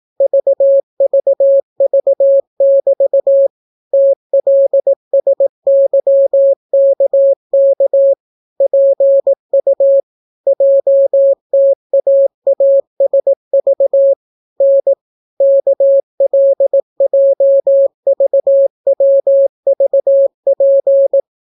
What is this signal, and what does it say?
Morse code, heard as a keyed tone, about 18 wpm, 550 Hz. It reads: VVV<BT> TLSYKK PU JTAASV N KLJVWVP